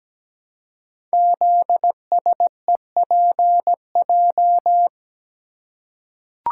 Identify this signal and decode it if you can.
Morse code, heard as a keyed tone, about 17 wpm, 700 Hz. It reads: ZSEPJ
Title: Code Groups - 5 Letters - 17wpm